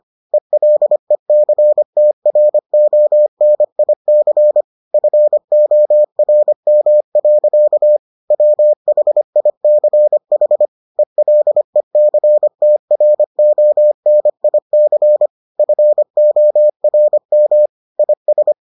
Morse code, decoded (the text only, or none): ELECTRONIC FORM. WHICH ELECTRONIC FORM IS